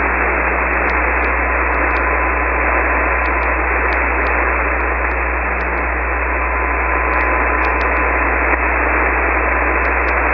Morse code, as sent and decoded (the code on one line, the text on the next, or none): none